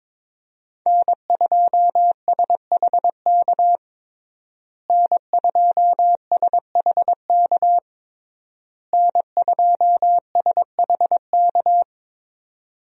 Morse code, heard as a keyed tone, about 21 wpm, 700 Hz. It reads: N2SHK N2SHK N2SHK
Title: USA Call Signs - 5 Characters - Repeated 3x - 22wpm